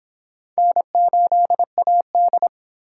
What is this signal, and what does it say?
Morse code, heard as a keyed tone, about 26 wpm, 700 Hz. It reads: N8AD